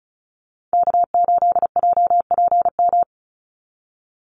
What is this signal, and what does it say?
Morse code, heard as a keyed tone, about 34 wpm, 700 Hz. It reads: K8JPM
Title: Call Signs Repeated 2x - 35wpm